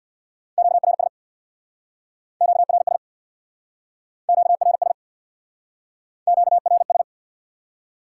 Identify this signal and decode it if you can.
Morse code, heard as a keyed tone, about 68 wpm, 700 Hz. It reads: QRS QRS QRS QRS